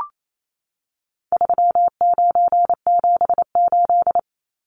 Morse code, heard as a keyed tone, about 28 wpm, 700 Hz. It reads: 3978
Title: Numbers - 4-Digit 28wpm